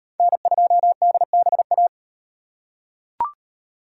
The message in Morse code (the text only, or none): N2DBA